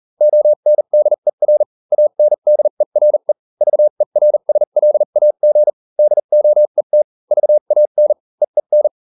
Morse code, heard as a keyed tone, about 40 wpm, spaced as for 35 wpm, 600 Hz. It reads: ONDER ANDERE VERSLAG DOET VAN EEN